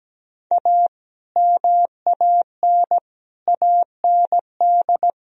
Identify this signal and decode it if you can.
Morse code, heard as a keyed tone, about 17 wpm, 700 Hz. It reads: A MAN AND
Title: Speed-Racing - Sets of 5 Words 17wpm